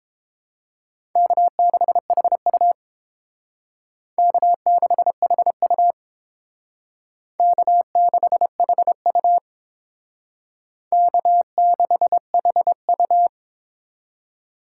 Morse code, heard as a keyed tone, about 28 wpm, 700 Hz. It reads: K6HU K6HU K6HU K6HU